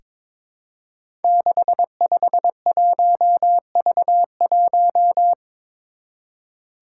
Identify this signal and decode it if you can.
Morse code, heard as a keyed tone, about 22 wpm, 700 Hz. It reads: 651V1
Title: Five Challenging Letter-Number Combinations 22wpm